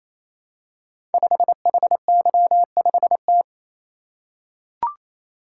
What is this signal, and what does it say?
Morse code, heard as a keyed tone, about 29 wpm, 700 Hz. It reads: 5HY5T
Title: Code Groups - 5 Letter:Numbers - 28wpm